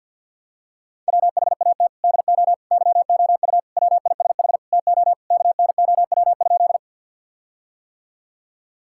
Morse code, heard as a keyed tone, about 50 wpm, 700 Hz. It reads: WHAT DO YOU WISH TO KNOW?
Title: Sentences from Top 1600 Words 50wpm